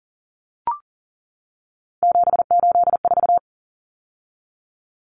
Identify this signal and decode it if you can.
Morse code, heard as a keyed tone, about 40 wpm, 700 Hz. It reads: 784
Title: Numbers - 3-Digit 40wpm